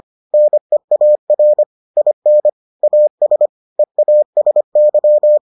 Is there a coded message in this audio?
Yes: Morse code from 0.3 to 5.4 s, heard as a keyed tone, about 25 wpm, 600 Hz. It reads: NEAR IN AS EASY